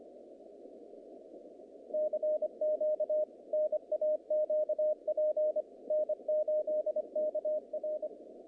Morse code, heard as a keyed tone, about 24 wpm, 610 Hz. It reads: CQ NAQP N8KR
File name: contests / naqp-cw-k5tr-op-k5pi- -2300